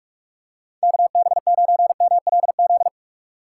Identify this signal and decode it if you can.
Morse code, heard as a keyed tone, about 46 wpm, 700 Hz. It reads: KB9MLZ
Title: USA Call Signs - 6 Characters - 2x Character Spacing - 45wpm